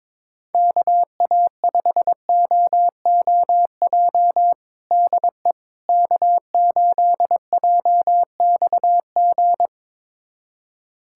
Morse code, heard as a keyed tone, about 22 wpm, 700 Hz. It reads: KA5OOJ DE K8JXG
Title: USA Callsign DE Callsign 22wpm